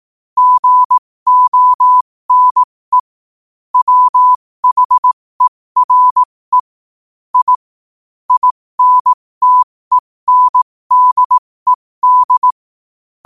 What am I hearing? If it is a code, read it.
Morse code, heard as a keyed tone, about 18 wpm, spaced as for 15 wpm, 1000 Hz. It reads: GONE WHERE I INTENDED
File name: qotdfiles / 15wpm